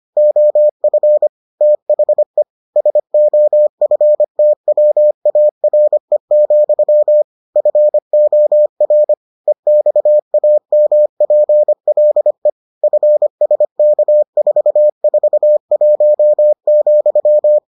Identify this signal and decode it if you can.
Morse code, heard as a keyed tone, about 25 wpm, 600 Hz. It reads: OF THE SOFTWARE, FOR EXAMPLE FSK441,